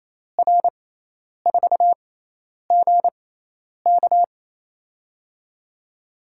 Morse code, heard as a keyed tone, about 28 wpm, spaced as for 10 wpm, 700 Hz. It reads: R4GK
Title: USA Call Signs - 5 Characters - 6x Character Spacing - 28wpm